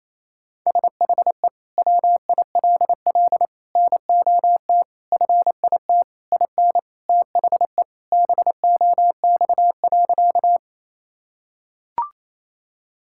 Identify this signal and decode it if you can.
Morse code, heard as a keyed tone, about 28 wpm, 700 Hz. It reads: SHE WILL NOT FIT IN THE BOX.
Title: Sentences from Top 1000 Words - Review Part 5 - 28wpm